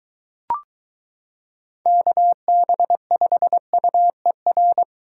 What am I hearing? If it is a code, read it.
Morse code, heard as a keyed tone, about 23 wpm, 700 Hz. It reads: KB5UER